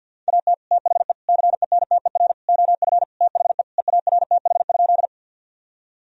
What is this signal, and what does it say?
Morse code, heard as a keyed tone, about 52 wpm, 700 Hz. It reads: AT THE CENTER OF THE EARTH?
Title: Sentences from Top 1000 Words - Review Part 2 - 50wpm